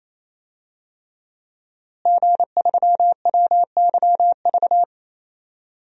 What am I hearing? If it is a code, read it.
Morse code, heard as a keyed tone, about 28 wpm, 700 Hz. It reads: G3WYV
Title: DXCC-100 Call Signs - 5 Characters - Repeated 3x - 28wpm